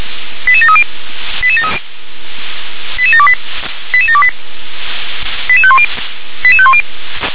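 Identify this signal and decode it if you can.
Five-tone selective call (ZVEI), heard as quick runs of five tones, tones of about 70 ms. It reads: 88730 88728 88728 88510 88510
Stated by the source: ZVEI five-tone selective call received on SDR.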